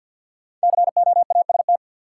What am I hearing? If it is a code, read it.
Morse code, heard as a keyed tone, about 46 wpm, 700 Hz. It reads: COAST
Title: Sentences from Top 1000 Words - Review Part 4 - New Word Drill - 50wpm